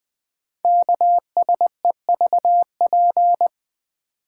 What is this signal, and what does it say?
Morse code, heard as a keyed tone, about 20 wpm, 700 Hz. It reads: KSEVP
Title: Code Groups - 5 Letter:Numbers - 20wpm